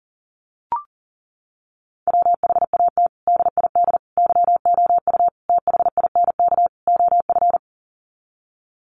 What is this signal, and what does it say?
Morse code, heard as a keyed tone, about 40 wpm, 700 Hz. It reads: WHAT DID YOU THINK OF
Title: Sets of 5 Words Encore2 40wpm